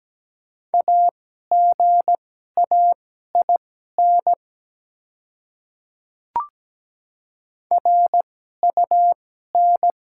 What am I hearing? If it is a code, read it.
Morse code, heard as a keyed tone, about 17 wpm, spaced as for 12 wpm, 690 Hz. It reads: AGAIN RUN
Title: 2x Character Spacing - Top 300 Words - 17wpm